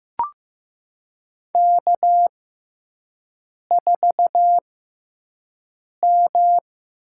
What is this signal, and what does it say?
Morse code, heard as a keyed tone, about 16 wpm, spaced as for 5 wpm, 700 Hz. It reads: K4M
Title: USA Call Signs - 6x Character Spacing - 15wpm